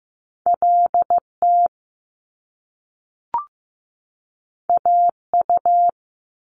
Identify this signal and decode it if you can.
Morse code, heard as a keyed tone, about 15 wpm, 700 Hz. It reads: LT AU
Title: Top Two Letter Combinations in Words 15wpm